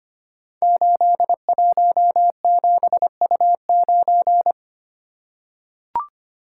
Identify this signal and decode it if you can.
Morse code, heard as a keyed tone, about 25 wpm, 700 Hz. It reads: 817U9
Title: Code Groups - 5 Letter:Numbers - 25wpm